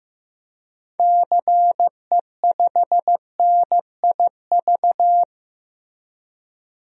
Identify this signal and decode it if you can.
Morse code, heard as a keyed tone, about 16 wpm, 700 Hz. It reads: CE5NIV
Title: DXCC-100 Call Signs - 6 Characters - Repeated 3x - 15wpm